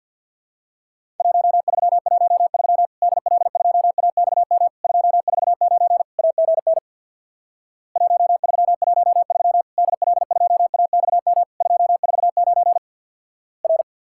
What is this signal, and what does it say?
Morse code, heard as a keyed tone, about 50 wpm, 700 Hz. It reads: 1313 DL2AXM 249 AGN 1313 DL2AXM 249 R
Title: Worked All Europe DX Contest - QTC Exchange with Repeats - 50wpm